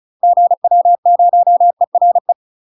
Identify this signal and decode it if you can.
Morse code, heard as a keyed tone, about 36 wpm, 700 Hz. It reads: GW0ERE
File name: UK MASTER QSOs TRAINING / 35WPM / UK MASTER QSOs 0041